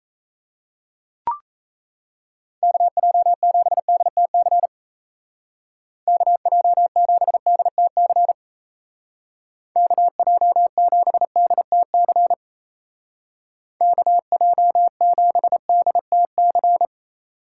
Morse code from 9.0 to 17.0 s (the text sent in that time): KJ7DTC KJ7DTC